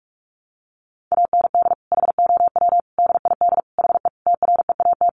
Morse code, heard as a keyed tone, about 44 wpm, 700 Hz. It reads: AND HOW DID HE TREAT
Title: Sentences from Top 1300 Words 45wpm